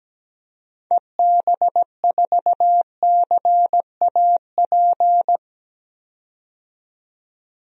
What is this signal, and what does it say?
Morse code, heard as a keyed tone, about 17 wpm, 700 Hz. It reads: EB4CAP
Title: DXCC-100 Call Signs - 6 Characters - 17wpm